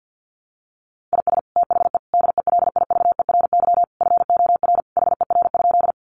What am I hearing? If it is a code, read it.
Morse code, heard as a keyed tone, about 50 wpm, 700 Hz. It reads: IS THE DELIVERY FOR HER?
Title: Sentences from Top 1500 Words 50wpm